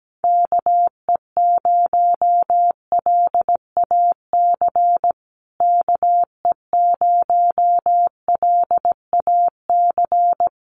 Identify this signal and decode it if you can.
Morse code, heard as a keyed tone, about 17 wpm, 700 Hz. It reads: KE0LAC KE0LAC
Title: USA Call Signs Repeated 3x 17wpm